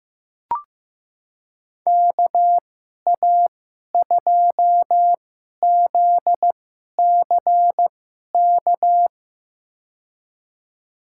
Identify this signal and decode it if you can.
Morse code, heard as a keyed tone, about 15 wpm, spaced as for 11 wpm, 700 Hz. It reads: KA2ZCK
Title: USA Call Signs - 6 Characters - 2x Character Spacing - 15wpm